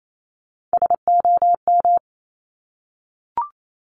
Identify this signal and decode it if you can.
Morse code, heard as a keyed tone, about 28 wpm, 690 Hz. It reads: SOM
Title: Top Three Letter Combinations in Words 28wpm